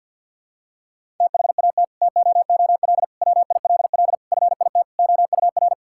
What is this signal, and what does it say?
Morse code, heard as a keyed tone, about 47 wpm, 700 Hz. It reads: THAT TOOL WILL FIT OUR
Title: Sentences from Top 1000 Words - Review Part 6 - 50wpm